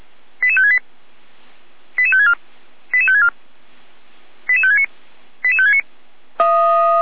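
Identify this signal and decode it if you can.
Five-tone selective call (ZVEI), heard as quick runs of five tones, tones of about 70 ms. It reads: 88568 88564 88564 88569 88569
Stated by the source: ZVEI five-tone selective call received by SDR.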